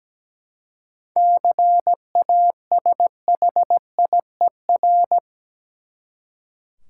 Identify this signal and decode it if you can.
Morse code, heard as a keyed tone, about 17 wpm, 700 Hz. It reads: CASHIER
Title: Practice QSO - Exchange 3 - More Info - 17wpm